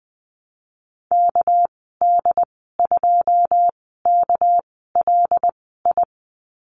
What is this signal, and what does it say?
Morse code, heard as a keyed tone, about 20 wpm, spaced as for 15 wpm, 700 Hz. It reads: KD2KLI
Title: 219 Lesson 35g - Number 0 - Call Signs 20wpm